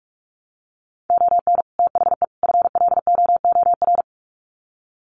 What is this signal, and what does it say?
Morse code, heard as a keyed tone, about 44 wpm, 700 Hz. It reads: ON THE FLOOR
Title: Sets of 3 Words 45wpm